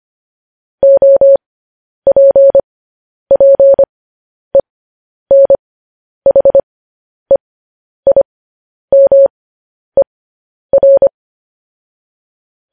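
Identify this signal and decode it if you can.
Morse code, heard as a keyed tone, about 25 wpm, spaced as for 10 wpm, 550 Hz. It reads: OPPENHEIMER
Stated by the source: Morse code news headlines transmission.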